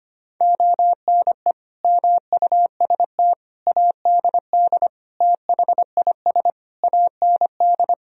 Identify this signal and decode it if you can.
Morse code, heard as a keyed tone, about 25 wpm, 700 Hz. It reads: ONE MUST ADD THIS AND
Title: Sentences from Top 1000 Words - Review Part 1 - 25wpm